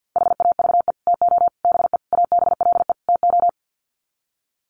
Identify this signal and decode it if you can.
Morse code, heard as a keyed tone, about 50 wpm, 700 Hz. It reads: HAVE TO BE ABLE TO
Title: Sets of 5 Words 50wpm